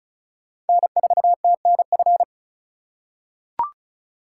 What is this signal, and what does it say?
Morse code, heard as a keyed tone, about 36 wpm, 700 Hz. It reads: N4TNF
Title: USA Call Signs - 5 Characters - 4x Character Spacing - 35wpm